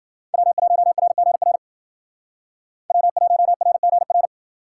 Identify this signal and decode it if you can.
Morse code, heard as a keyed tone, about 52 wpm, 700 Hz. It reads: W1RGR W1RGR